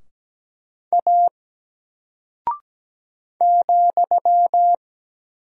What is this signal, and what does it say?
Morse code, heard as a keyed tone, about 17 wpm, 700 Hz. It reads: A ,
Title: Single Character (Letter:Number:Punctuation:Prosign) Alt 17wpm